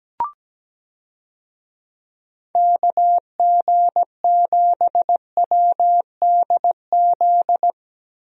Morse code, heard as a keyed tone, about 17 wpm, 700 Hz. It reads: KG7WDZ
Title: USA Call Signs - 6 Characters - 17wpm